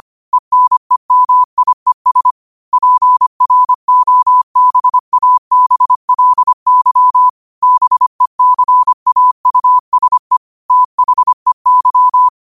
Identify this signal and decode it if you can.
Morse code, heard as a keyed tone, about 25 wpm, 1000 Hz. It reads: ENEMIES PROBABLY BECAUSE THEY
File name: qotdfiles / 25wpm